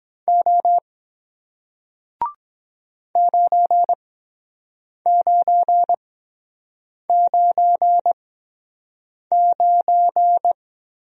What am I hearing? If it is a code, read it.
Morse code, heard as a keyed tone, about 24 wpm, spaced as for 7 wpm, 700 Hz. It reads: O 9999